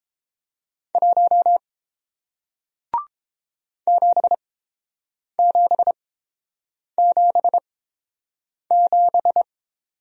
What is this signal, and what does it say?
Morse code, heard as a keyed tone, about 30 wpm, spaced as for 7 wpm, 700 Hz. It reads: 1 7777